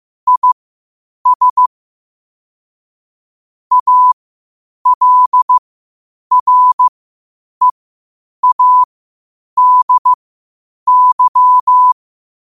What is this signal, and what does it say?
Morse code, heard as a keyed tone, about 15 wpm, spaced as for 8 wpm, 1000 Hz. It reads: IS ALREADY
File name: qotdfiles / 8wpm